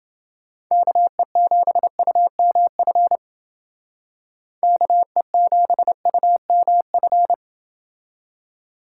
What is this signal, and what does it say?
Morse code, heard as a keyed tone, about 28 wpm, 700 Hz. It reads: KE7UMF KE7UMF